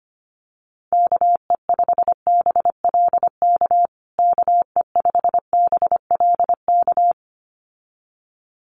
Morse code, heard as a keyed tone, about 25 wpm, 700 Hz. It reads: KE5BLK KE5BLK
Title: USA Call Signs Repeated 2x 25wpm